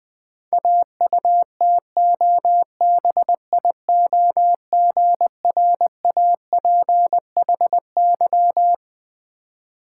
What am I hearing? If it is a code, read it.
Morse code, heard as a keyed tone, about 20 wpm, 700 Hz. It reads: AUTOBIOGRAPHY
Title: Difficult Words with Hints - 20wpm